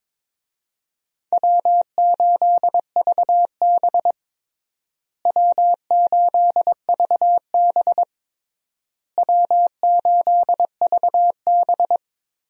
Morse code, heard as a keyed tone, about 22 wpm, 700 Hz. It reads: W8VB W8VB W8VB
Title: USA Call Signs - 4 Characters - Repeated 3x - 22wpm